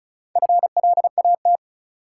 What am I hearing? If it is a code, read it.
Morse code, heard as a keyed tone, about 36 wpm, 700 Hz. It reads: FLAT
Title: Sentences from Top 1000 Words - New Word Drill - 35wpm